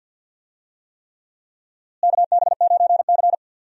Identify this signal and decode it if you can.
Morse code, heard as a keyed tone, about 47 wpm, 700 Hz. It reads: KB9C
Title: USA Call Signs - 4 Characters - Repeated 3x - 50wpm